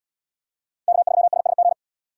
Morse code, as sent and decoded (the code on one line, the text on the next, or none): -.- ...-- ... ... --.
K3SSG